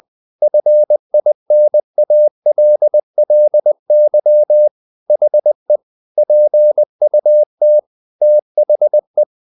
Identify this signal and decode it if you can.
Morse code, heard as a keyed tone, about 20 wpm, 600 Hz. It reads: FINALLY HE PUT THE
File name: the old man and the sea part 2 20wpm